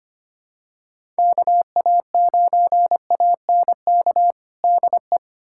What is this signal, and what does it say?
Morse code, heard as a keyed tone, about 25 wpm, 700 Hz. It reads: KA9ANK DE
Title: USA Callsign DE Callsign - 6 Characters - 25wpm